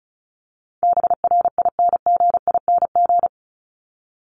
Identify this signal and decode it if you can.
Morse code, heard as a keyed tone, about 34 wpm, 700 Hz. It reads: BRINGING
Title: ING Suffix - Top 100 Words with Base Words 35wpm